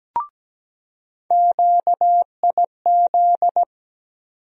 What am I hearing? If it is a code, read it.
Morse code, heard as a keyed tone, about 17 wpm, 700 Hz. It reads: QIZ
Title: USA Call Sign Postfixes - 3 Letters 17wpm